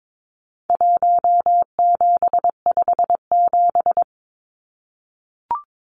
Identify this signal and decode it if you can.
Morse code, heard as a keyed tone, about 22 wpm, 700 Hz. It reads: 1757
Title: Numbers - Contest Style - 4-Digit 22wpm